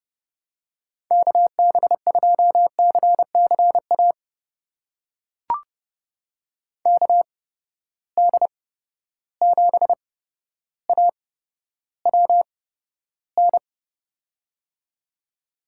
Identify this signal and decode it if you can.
Morse code, heard as a keyed tone, about 30 wpm, spaced as for 13 wpm, 700 Hz. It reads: KB2CCA KD7AWN
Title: USA Call Signs - 8x Character Spacing - 30wpm